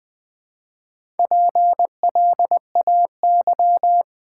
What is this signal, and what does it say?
Morse code, heard as a keyed tone, about 20 wpm, 690 Hz. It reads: PLAY
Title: Build A Word - 1 Letter At A Time - 4-Letter Words - 20wpm